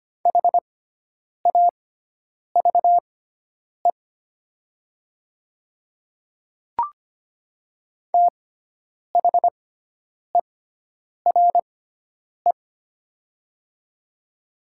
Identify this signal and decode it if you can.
Morse code, heard as a keyed tone, about 25 wpm, spaced as for 9 wpm, 700 Hz. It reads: HAVE THERE
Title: 6x Character Spacing - Top 100 Words - 25wpm